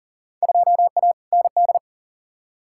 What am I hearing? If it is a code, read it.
Morse code, heard as a keyed tone, about 40 wpm, 700 Hz. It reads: 2A ND
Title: Field Day Exchanges - 40wpm